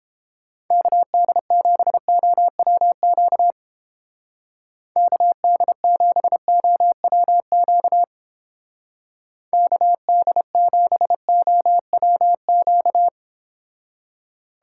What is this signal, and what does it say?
Morse code, heard as a keyed tone, about 29 wpm, 700 Hz. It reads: KD7OWQ KD7OWQ KD7OWQ